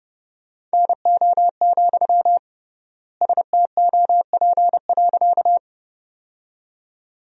Morse code, heard as a keyed tone, about 32 wpm, 700 Hz. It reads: NO, STOP.
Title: 3x Word-Spacing - Sentences from Top 300 Words - 30wpm